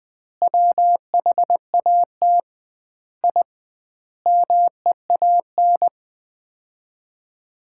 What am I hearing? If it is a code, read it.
Morse code, heard as a keyed tone, about 20 wpm, 700 Hz. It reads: WHAT I MEAN
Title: 2x Word-Spacing - Sets of 5 Words - 20wpm